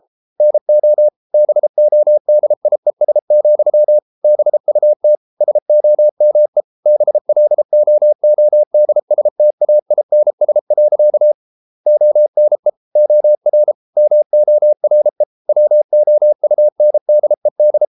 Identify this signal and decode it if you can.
Morse code, heard as a keyed tone, about 33 wpm, 600 Hz. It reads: NO BODIES, BUT SOME BLOODSTAINS. ONE OR MORE WOUNDED